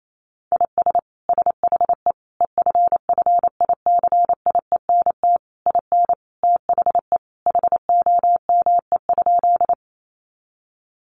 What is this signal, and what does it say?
Morse code, heard as a keyed tone, about 28 wpm, 700 Hz. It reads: IS SHE EFFICIENT IN THE HOME?
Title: Sentences from Top 1300 Words 28wpm